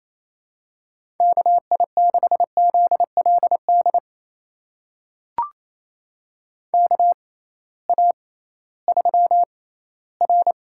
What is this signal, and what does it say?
Morse code, heard as a keyed tone, about 28 wpm, spaced as for 16 wpm, 700 Hz. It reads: KI6ZLD KA3R
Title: USA Call Signs - 6 Characters - 6x Character Spacing - 28wpm